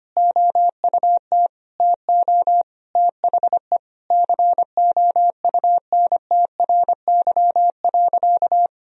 Morse code, heard as a keyed tone, about 25 wpm, 700 Hz. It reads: OUT TO THE COUNTRY.
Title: Sentences from Top 1000 Words - Review Part 1 - 25wpm